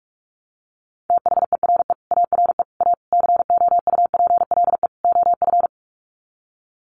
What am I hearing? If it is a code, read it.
Morse code, heard as a keyed tone, about 44 wpm, 700 Hz. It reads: THERE ARE A COUPLE OF
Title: Sets of 5 Words Encore 45wpm